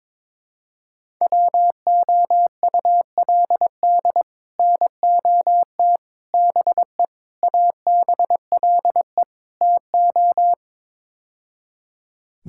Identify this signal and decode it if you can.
Morse code, heard as a keyed tone, about 22 wpm, 700 Hz. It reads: WOULD NOT BE ABLE TO
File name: practice / 22wpm / Build A Phrase - 5-Word Phrases - 22wpm